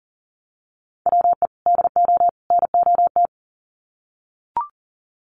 Morse code, heard as a keyed tone, about 37 wpm, 700 Hz. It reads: WE DO NOT
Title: Sets of 3 Words 40wpm